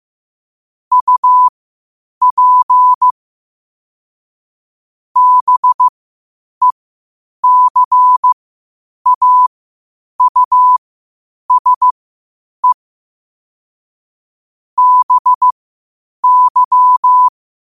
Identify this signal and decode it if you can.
Morse code, heard as a keyed tone, about 15 wpm, spaced as for 8 wpm, 1000 Hz. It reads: UP BECAUSE BY
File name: qotdfiles / 8wpm